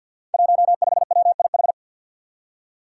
Morse code, heard as a keyed tone, about 51 wpm, 700 Hz. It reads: 15WIH
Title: Five Challenging Letter-Number Combinations 50wpm